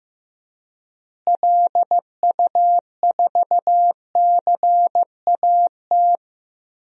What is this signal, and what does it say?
Morse code, heard as a keyed tone, about 15 wpm, 700 Hz. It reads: LU4CAT
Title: DXCC-100 Call Signs - 6 Characters - Repeated 2x - 15wpm